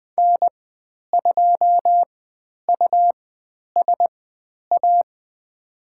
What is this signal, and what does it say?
Morse code, heard as a keyed tone, about 20 wpm, spaced as for 10 wpm, 700 Hz. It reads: N2USA
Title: 109 Lesson 18g - Letter m - Call Signs 20wpm